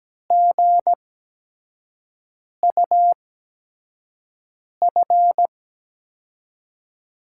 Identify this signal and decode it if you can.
Morse code, heard as a keyed tone, about 17 wpm, spaced as for 5 wpm, 700 Hz. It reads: GUF
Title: USA Call Signs - 8x Character Spacing - 17wpm